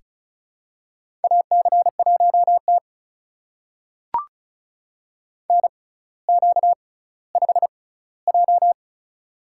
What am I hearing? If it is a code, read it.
Morse code, heard as a keyed tone, about 34 wpm, spaced as for 18 wpm, 700 Hz. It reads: AC1T NQ5J
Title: USA Call Signs - 4 Characters - 6x Character Spacing - 35wpm